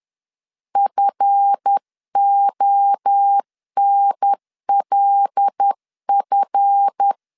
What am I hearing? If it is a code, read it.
Morse code, heard as a keyed tone, about 10 wpm, 800 Hz. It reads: FONLF
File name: LEZIONI / 7 GF9Z / 7 GF 10wpm 2514